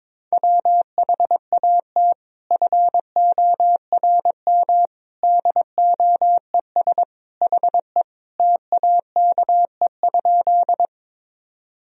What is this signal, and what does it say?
Morse code, heard as a keyed tone, about 22 wpm, 700 Hz. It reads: WHAT FORM DOES HE TAKE?
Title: Sentences from Top 1000 Words - Review Part 1 - 22wpm